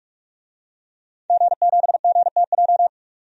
Morse code, heard as a keyed tone, about 46 wpm, 700 Hz. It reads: G7GTJ
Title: DXCC-100 Call Signs - 5 Characters - 45wpm